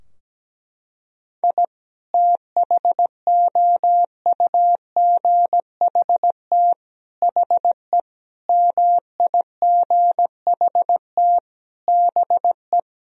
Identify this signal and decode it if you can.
Morse code, heard as a keyed tone, about 17 wpm, 700 Hz. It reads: I THOUGHT HE MIGHT BE
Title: Sentences from Top 1000 Words - Review Part 4 - 17wpm